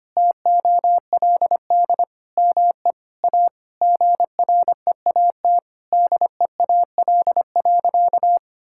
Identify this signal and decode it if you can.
Morse code, heard as a keyed tone, about 25 wpm, 700 Hz. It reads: TOLD ME A GREAT DEAL.
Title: Sentences from Top 1000 Words - Review Part 6 - 25wpm